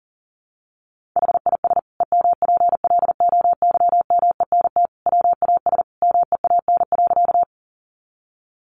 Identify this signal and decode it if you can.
Morse code, heard as a keyed tone, about 40 wpm, 700 Hz. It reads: HIS EMPLOYMENT WAS MEAN.
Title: Sentences from Top 1500 Words 40wpm